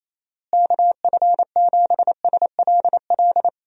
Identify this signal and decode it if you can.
Morse code, heard as a keyed tone, about 28 wpm, 700 Hz. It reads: KF7SLL
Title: USA Call Signs - 6 Characters - Repeated 2x - 28wpm